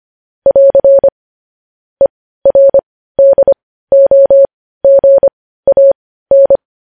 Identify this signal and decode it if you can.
Morse code, heard as a keyed tone, about 25 wpm, spaced as for 15 wpm, 550 Hz. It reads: <AR> ERDOGAN
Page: News Headlines in Morse Code at 15 WPM